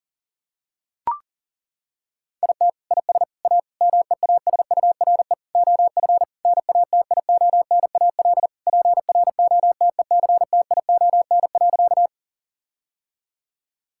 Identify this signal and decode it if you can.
Morse code, heard as a keyed tone, about 40 wpm, 700 Hz. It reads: IT IS A MEASURE OF NATIONAL PROTECTION.
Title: Sentences from Top 1600 Words 40wpm